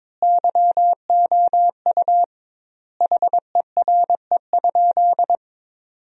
Morse code, heard as a keyed tone, about 22 wpm, 700 Hz. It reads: YOU HERE?
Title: 2x Word-Spacing - Sentences from Top 500 Words - 22wpm